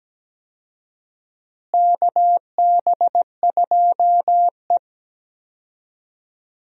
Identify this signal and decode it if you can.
Morse code, heard as a keyed tone, about 17 wpm, 700 Hz. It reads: KB2E
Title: USA Call Signs - 4 Characters - Repeated 3x - 17wpm